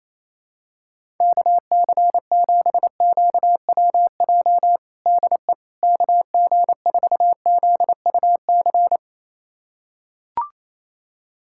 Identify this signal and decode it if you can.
Morse code, heard as a keyed tone, about 28 wpm, 700 Hz. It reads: KC7QWJ DE KG4ZUC
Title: USA Callsign DE Callsign - 6 Characters - 28wpm